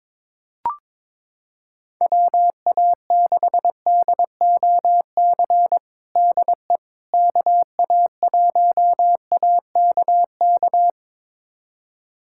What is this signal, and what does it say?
Morse code, heard as a keyed tone, about 22 wpm, 700 Hz. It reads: WA6DOC DE KA1AKK
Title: USA Callsign DE Callsign - 6 Characters - 22wpm